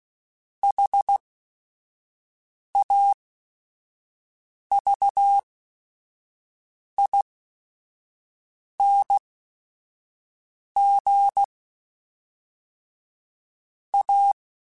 Morse code, heard as a keyed tone, about 16 wpm, spaced as for 5 wpm, 800 Hz. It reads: HAVING A
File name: Quotes / Quotes-mp3s / ChildAsleep @5